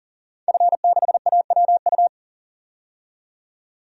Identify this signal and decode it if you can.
Morse code, heard as a keyed tone, about 40 wpm, 700 Hz. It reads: F6AWU
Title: DXCC-100 Call Signs - 5 Characters - 40wpm